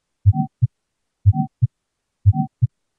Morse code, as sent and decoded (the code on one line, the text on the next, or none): none